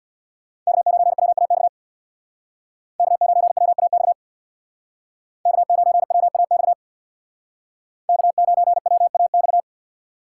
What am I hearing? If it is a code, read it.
Morse code, heard as a keyed tone, about 58 wpm, 700 Hz. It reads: K9WAX K9WAX K9WAX K9WAX
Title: Speed Racing - USA 1x3 Call Signs 50wpm